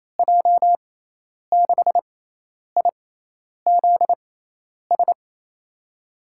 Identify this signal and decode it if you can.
Morse code, heard as a keyed tone, about 28 wpm, spaced as for 10 wpm, 700 Hz. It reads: J6IZS
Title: USA Call Signs - 6 Characters - 6x Character Spacing - 28wpm